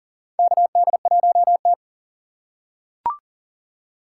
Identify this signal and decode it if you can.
Morse code, heard as a keyed tone, about 40 wpm, 700 Hz. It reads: KD1T
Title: USA Call Signs - 4 Characters - 3x Character Spacing - 40wpm